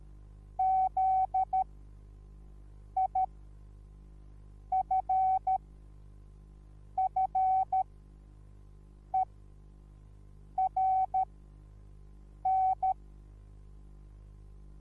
Morse code, heard as a keyed tone, about 13 wpm, spaced as for 5 wpm, 760 Hz. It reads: ZIFFERN
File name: Onlinelehrgang / morsen / DJ4UF Morsen21